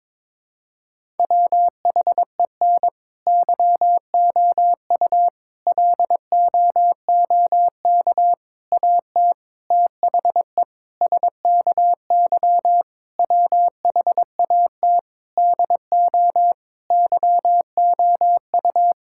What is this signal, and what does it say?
Morse code, heard as a keyed tone, about 22 wpm, 700 Hz. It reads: WHEN YOU LOOK AT THE SKY WHAT DO YOU
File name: practice / 22wpm / Sentences with 3-Word Phrases - 22wpm